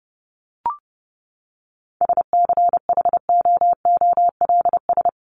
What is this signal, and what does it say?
Morse code, heard as a keyed tone, about 30 wpm, 700 Hz. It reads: SCHOOLS
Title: ES:S Suffix - Top 100 Words with Base Words 30wpm